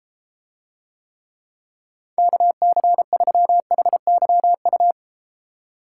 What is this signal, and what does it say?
Morse code, heard as a keyed tone, about 33 wpm, 700 Hz. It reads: KC3HYU